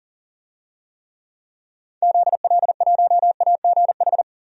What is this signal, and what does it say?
Morse code, heard as a keyed tone, about 38 wpm, 700 Hz. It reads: ZL1AGH
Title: DXCC-100 Call Signs - 6 Characters - 40wpm